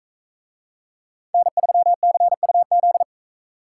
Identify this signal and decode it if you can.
Morse code, heard as a keyed tone, about 40 wpm, 700 Hz. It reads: N3CUZ